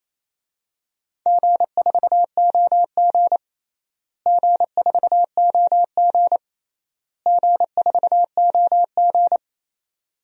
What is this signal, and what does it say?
Morse code, heard as a keyed tone, about 28 wpm, 700 Hz. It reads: G4OG G4OG G4OG
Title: DXCC-100 Call Signs - 4 Characters - Repeated 3x - 28wpm